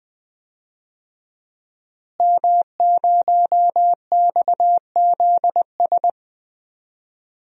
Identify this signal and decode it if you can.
Morse code, heard as a keyed tone, about 20 wpm, 700 Hz. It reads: M0XZS
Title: DXCC-100 Call Signs - 5 Characters - Repeated 2x - 20wpm